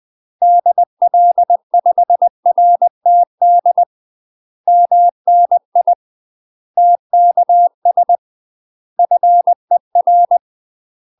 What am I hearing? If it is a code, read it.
Morse code, heard as a keyed tone, about 20 wpm, 700 Hz. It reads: DL5RTD MNI TKS FER
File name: UK MASTER QSOs TRAINING / 20WPM / UK MASTER QSOs 0008